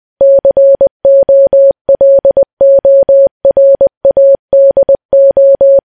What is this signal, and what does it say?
Morse code, heard as a keyed tone, about 20 wpm, 550 Hz. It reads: COLORADO
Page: news-headlines-in-morse-code-at-20-wpm